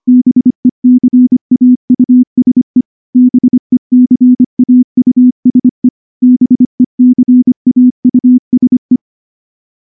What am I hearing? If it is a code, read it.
Morse code, heard as a keyed tone, about 24 wpm, 250 Hz. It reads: BECAUSE BECAUSE BECAUSE
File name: morse / morse-training-25WPM